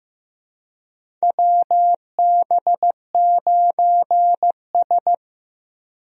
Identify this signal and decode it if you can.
Morse code, heard as a keyed tone, about 15 wpm, 700 Hz. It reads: WB9S
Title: USA Call Signs - 4 Characters - Repeated 2x - 15wpm